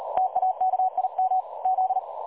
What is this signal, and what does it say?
Morse code, heard as a keyed tone, about 37 wpm, 750 Hz. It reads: TAKIM Z
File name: dlazaawansowanych / nagraniazpasma / websdr recording 3565.0kHz